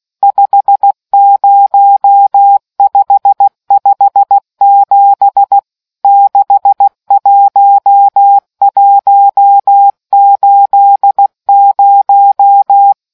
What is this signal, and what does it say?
Morse code, heard as a keyed tone, about 16 wpm, 800 Hz. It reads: 50557 61180